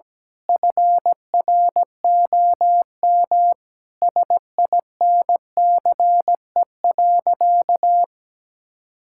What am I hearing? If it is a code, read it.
Morse code, heard as a keyed tone, about 17 wpm, 700 Hz. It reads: FROM SINCE.
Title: Sentences from Top 1600 Words 17wpm